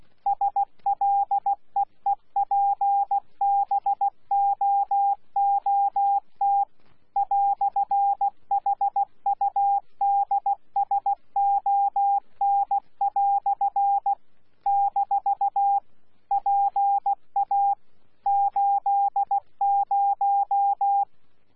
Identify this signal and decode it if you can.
Morse code, heard as a keyed tone, about 16 wpm, 800 Hz. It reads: SLEEPBOOT "HUDSON" - PA80
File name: Z PI4VRZ / PI4VRZ 14-jun-2025